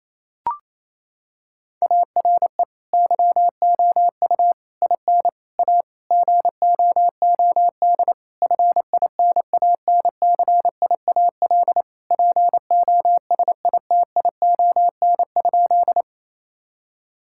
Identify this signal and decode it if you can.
Morse code, heard as a keyed tone, about 28 wpm, 700 Hz. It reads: ARE YOU IN A GOOD FINANCIAL POSITION?
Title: Sentences from Top 1000 Words - Review Part 5 - 28wpm